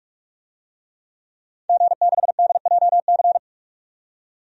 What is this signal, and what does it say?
Morse code, heard as a keyed tone, about 46 wpm, 700 Hz. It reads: G6DJC
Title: DXCC-100 Call Signs - 5 Characters - 45wpm